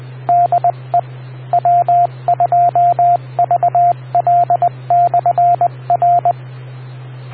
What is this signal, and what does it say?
Morse code, heard as a keyed tone, about 20 wpm, 700 Hz, with radio noise. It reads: DE W2VL/R